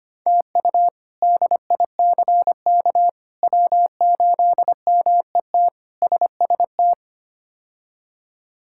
Morse code, heard as a keyed tone, about 25 wpm, 700 Hz. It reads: TU DICK W8MET SST
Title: K1USN SST QSO Practice - 25wpm